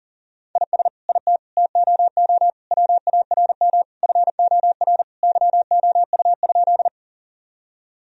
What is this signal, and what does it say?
Morse code, heard as a keyed tone, about 40 wpm, 700 Hz. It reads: IS IT TOO WARM FOR YOU?
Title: Sentences from Top 1000 Words - Review Part 2 - 40wpm